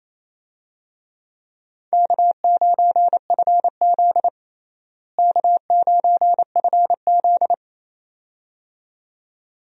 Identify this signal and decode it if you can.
Morse code, heard as a keyed tone, about 28 wpm, 700 Hz. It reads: K9FZ K9FZ
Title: USA Call Signs - 4 Characters - Repeated 2x - 28wpm